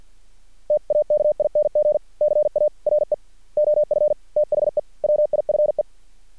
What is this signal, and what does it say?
Morse code, heard as a keyed tone, about 50 wpm, 600 Hz. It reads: TAKING CARE OF THE WIFE
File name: archived / CRSnet